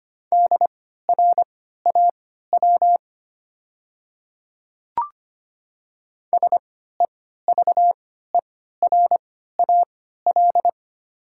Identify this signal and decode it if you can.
Morse code, heard as a keyed tone, about 25 wpm, spaced as for 14 wpm, 700 Hz. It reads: DRAW SEVERAL
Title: 3x Character Spacing - Top 500 Words - 25wpm